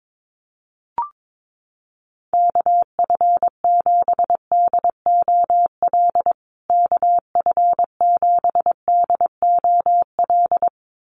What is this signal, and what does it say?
Morse code, heard as a keyed tone, about 22 wpm, 700 Hz. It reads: KF7DOL KF7DOL
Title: USA Call Signs Repeated 2x 22wpm